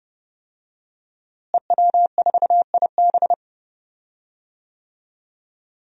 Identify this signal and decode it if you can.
Morse code, heard as a keyed tone, about 31 wpm, 700 Hz. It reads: EW4IB
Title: DXCC-100 Call Signs - 5 Characters - Repeated 3x - 30wpm